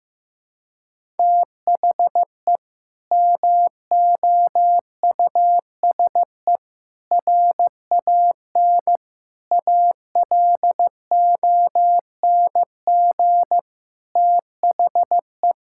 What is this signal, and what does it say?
Morse code, heard as a keyed tone, about 15 wpm, 700 Hz. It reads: THE MOUSE RAN ALONG THE
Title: Sentences from Top 1000 Words - Review Part 5 - 15wpm